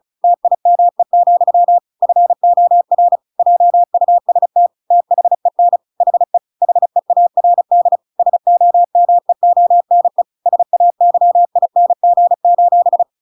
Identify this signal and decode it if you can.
Morse code, heard as a keyed tone, about 35 wpm, 700 Hz. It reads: TIME, FOR JUST THEN HE HEARD SOMEONE SAYING: